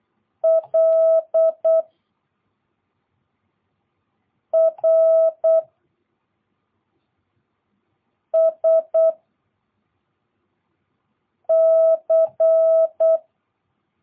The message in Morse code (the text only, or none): L R S C